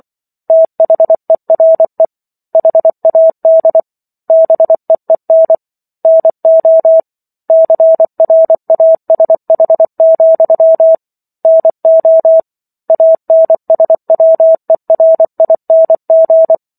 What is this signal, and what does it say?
Morse code, heard as a keyed tone, about 24 wpm, 650 Hz. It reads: THERE HAD BEEN NO CRASH, NO ANSWERING